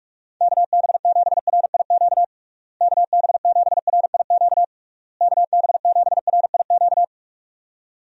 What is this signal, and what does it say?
Morse code, heard as a keyed tone, about 41 wpm, 700 Hz. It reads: KB7RIQ KB7RIQ KB7RIQ
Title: USA Call Signs - 6 Characters - Repeated 3x - 45wpm